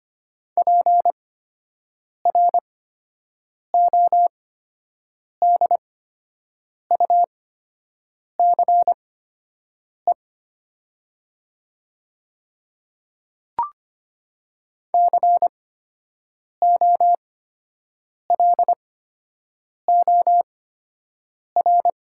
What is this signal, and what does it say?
Morse code, heard as a keyed tone, about 25 wpm, spaced as for 7 wpm, 700 Hz. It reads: PRODUCE COLOR
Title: 8x Character Spacing - Top 500 Words - 25wpm